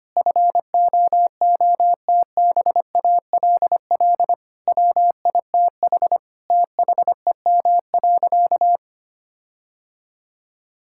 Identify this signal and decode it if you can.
Morse code, heard as a keyed tone, about 25 wpm, 700 Hz. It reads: FOOTBALL WITH THEM.
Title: Sentences from Top 1600 Words 25wpm